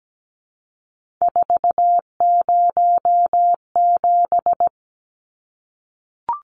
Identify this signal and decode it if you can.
Morse code, heard as a keyed tone, about 17 wpm, 700 Hz. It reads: 407
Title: Numbers - 3-Digit 17wpm